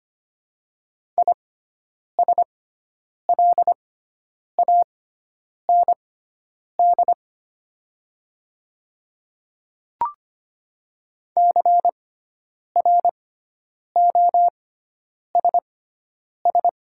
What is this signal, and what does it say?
Morse code, heard as a keyed tone, about 25 wpm, spaced as for 9 wpm, 700 Hz. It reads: ISLAND CROSS
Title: 6x Character Spacing - Top 500 Words - 25wpm